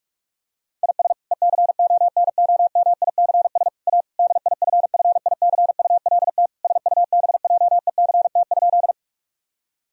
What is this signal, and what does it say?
Morse code, heard as a keyed tone, about 46 wpm, 700 Hz. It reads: IS ECONOMICS A DIFFICULT SUBJECT?
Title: Sentences from Top 1000 Words - Review Part 3 - 45wpm